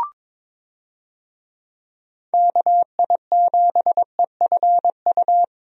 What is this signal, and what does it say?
Morse code, heard as a keyed tone, about 22 wpm, 700 Hz. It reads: KI7EFU
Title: USA Call Signs - 6 Characters - 22wpm